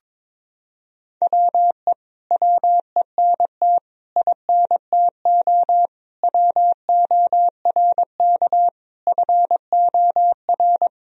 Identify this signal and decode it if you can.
Morse code, heard as a keyed tone, about 21 wpm, 700 Hz. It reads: WE WENT INTO WORK FOR
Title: Sentences from Top 1000 Words - Review Part 4 - 22wpm